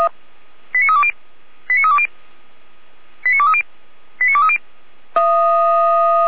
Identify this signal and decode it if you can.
Five-tone selective call (ZVEI), heard as quick runs of five tones, tones of about 70 ms. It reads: 89320 89320 89230 89230
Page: /455/SDR/ZVEI/89015/